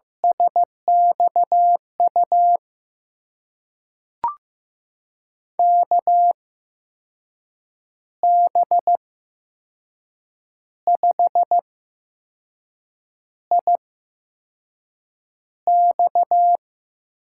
Morse code, heard as a keyed tone, about 15 wpm, spaced as for 5 wpm, 700 Hz. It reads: SXU KB5IX
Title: USA Call Signs - 6 Characters - 8x Character Spacing - 15wpm